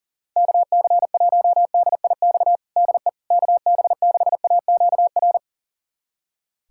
Morse code, heard as a keyed tone, about 40 wpm, 700 Hz. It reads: KC1DIX DE KB6AQR
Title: USA Callsign DE Callsign 40wpm